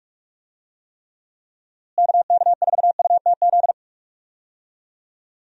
Morse code, heard as a keyed tone, about 46 wpm, 700 Hz. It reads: KK4UTZ
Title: USA Call Signs - 6 Characters - Repeated 2x - 45wpm